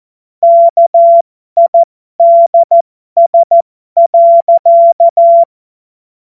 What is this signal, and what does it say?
Morse code, heard as a keyed tone, about 14 wpm, spaced as for 12 wpm, 680 Hz. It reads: KIDS.
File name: Morsekurs / Witze